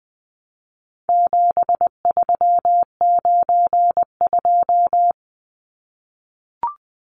Morse code, heard as a keyed tone, about 20 wpm, 700 Hz. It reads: 7392
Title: Numbers - 4-Digit 20wpm